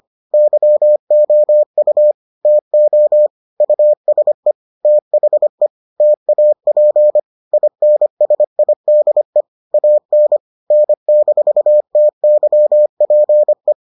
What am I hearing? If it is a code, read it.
Morse code, heard as a keyed tone, about 25 wpm, 600 Hz. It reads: YOU TO USE THE TAP INSIDE AN N-TYPE